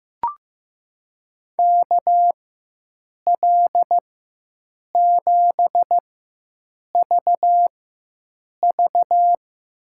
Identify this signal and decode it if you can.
Morse code, heard as a keyed tone, about 15 wpm, spaced as for 7 wpm, 700 Hz. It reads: KL7VV
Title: USA Call Signs - 5 Characters - 4x Character Spacing - 15wpm